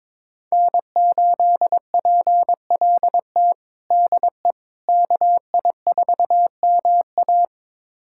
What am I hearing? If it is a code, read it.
Morse code, heard as a keyed tone, about 22 wpm, 700 Hz. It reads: N8PLT DE KI4MA
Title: USA Callsign DE Callsign - 5 Characters - 22wpm